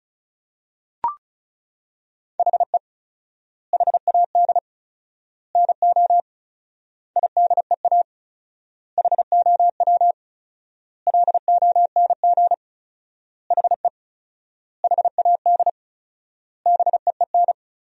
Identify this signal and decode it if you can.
Morse code, heard as a keyed tone, about 36 wpm, 700 Hz. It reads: HE HAD NO IDEA HOW LONG HE HAD BEEN
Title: 4x Word-Spacing - Sentences from Top 300 Words - 35wpm